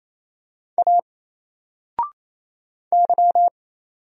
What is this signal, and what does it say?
Morse code, heard as a keyed tone, about 27 wpm, 700 Hz. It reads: A Y
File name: Single-Letter-Number-28wpm